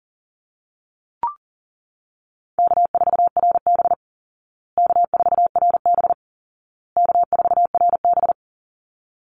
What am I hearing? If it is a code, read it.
Morse code, heard as a keyed tone, about 40 wpm, 700 Hz. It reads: K4RB K4RB K4RB
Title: Easy Call Signs Repeated 3x - 40wpm